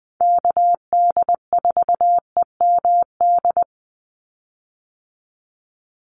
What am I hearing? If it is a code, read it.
Morse code, heard as a keyed tone, about 20 wpm, 700 Hz. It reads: KD4EMD
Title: USA Call Signs 20wpm